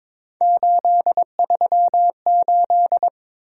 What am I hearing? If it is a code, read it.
Morse code, heard as a keyed tone, about 22 wpm, 700 Hz. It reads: 838
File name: practice / 22wpm / Three Letter-Number Combinations - 22wpm